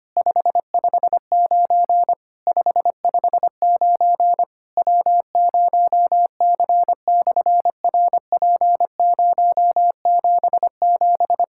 Morse code, heard as a keyed tone, about 25 wpm, 700 Hz. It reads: 559 559 W0C/RP077
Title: SOTA QSO Practice 25wpm